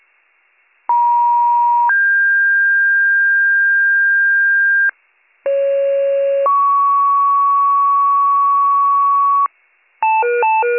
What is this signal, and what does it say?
Two-tone page, heard as a one-second tone, then a three-second tone, about 950 then 1600 Hz, about 550 then 1120 Hz, with noise. It is heard from 0.9 to 9.5 s.